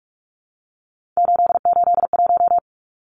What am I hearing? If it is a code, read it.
Morse code, heard as a keyed tone, about 44 wpm, 700 Hz. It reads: 881
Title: Numbers - 3-Digit 45wpm